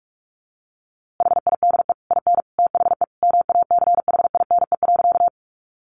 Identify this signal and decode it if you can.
Morse code, heard as a keyed tone, about 44 wpm, 700 Hz. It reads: HIDE IN THE MACHINE.
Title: A Sound Of Thunder 45wpm